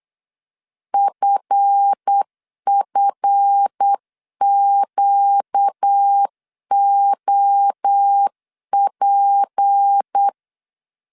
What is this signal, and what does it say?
Morse code, heard as a keyed tone, about 8 wpm, 800 Hz. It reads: FFQOP
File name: LEZIONI / 7 GF9Z / 7 GF 8wpm 3108